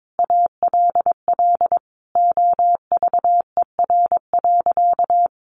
Morse code, heard as a keyed tone, about 22 wpm, 700 Hz. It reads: ALL OVER.
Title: Sentences from Top 1000 Words 22wpm